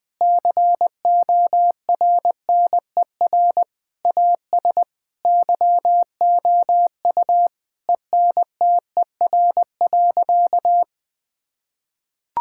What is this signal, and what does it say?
Morse code, heard as a keyed tone, about 20 wpm, 700 Hz. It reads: CORNER AS YOU ENTER.
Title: Sentences from Top 1600 Words 20wpm